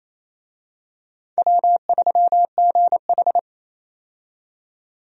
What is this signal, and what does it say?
Morse code, heard as a keyed tone, about 27 wpm, 700 Hz. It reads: W3GH
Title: USA Call Signs - 4 Characters - 28wpm